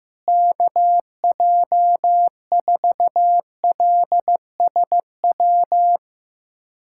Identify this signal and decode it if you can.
Morse code, heard as a keyed tone, about 15 wpm, 700 Hz. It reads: KJ4LSW
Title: USA Callsign DE Callsign 15wpm